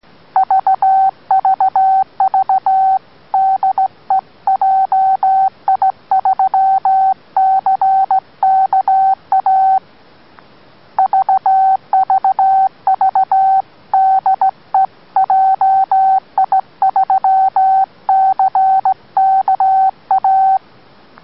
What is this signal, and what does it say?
Morse code, heard as a keyed tone, about 16 wpm, 770 Hz. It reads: VVV DE JI3CKA VVV DE JI3CKA